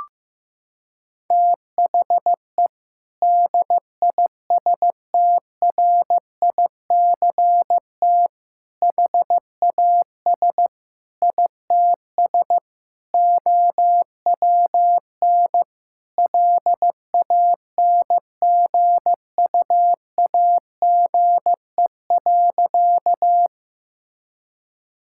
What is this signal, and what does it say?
Morse code, heard as a keyed tone, about 15 wpm, 700 Hz. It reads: THE DISTRICT HAS ITS OWN LANGUAGE.
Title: Sentences from Top 1600 Words 15wpm